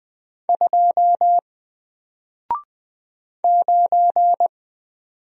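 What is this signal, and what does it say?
Morse code, heard as a keyed tone, about 20 wpm, 700 Hz. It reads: 2 9
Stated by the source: Single letter and number Morse drill at 20 wpm, from N4LCD.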